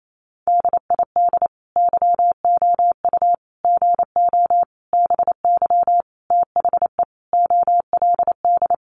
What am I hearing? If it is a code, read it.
Morse code, heard as a keyed tone, about 28 wpm, 700 Hz. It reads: DID YOU GO BY THE OLD
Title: Sentences from Top 1200 Words 28wpm